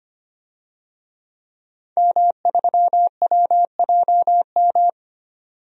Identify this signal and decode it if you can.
Morse code, heard as a keyed tone, about 25 wpm, 700 Hz. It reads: M3WJM
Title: DXCC-100 Call Signs - 5 Characters - 25wpm